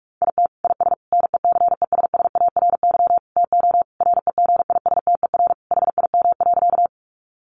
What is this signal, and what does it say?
Morse code, heard as a keyed tone, about 44 wpm, 700 Hz. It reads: IT IS NECESSARY TO REGISTER HIM.
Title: Sentences from Top 1200 Words 45wpm